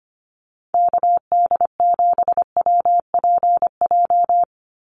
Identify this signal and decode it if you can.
Morse code, heard as a keyed tone, about 25 wpm, 690 Hz. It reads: KD7WPJ
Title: NA SOTA Activator Callsigns 25wpm